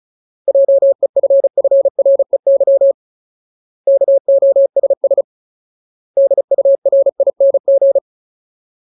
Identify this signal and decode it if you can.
Morse code, heard as a keyed tone, about 34 wpm, 550 Hz. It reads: JEFFREY KOSS DURING
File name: Ultimate Word List / 35WPM Ultimate